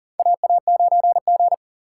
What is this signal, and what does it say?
Morse code, heard as a keyed tone, about 40 wpm, 700 Hz. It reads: AA9G
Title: USA Call Signs - 4 Characters - 3x Character Spacing - 40wpm